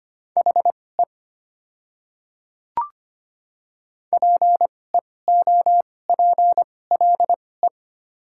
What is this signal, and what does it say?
Morse code, heard as a keyed tone, about 25 wpm, spaced as for 18 wpm, 690 Hz. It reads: HE PEOPLE
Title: 2x Character Spacing - Top 100 Words - 25wpm